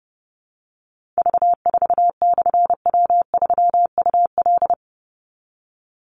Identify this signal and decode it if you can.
Morse code, heard as a keyed tone, about 30 wpm, 700 Hz. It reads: V4/W3UL
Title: Call Signs - 30wpm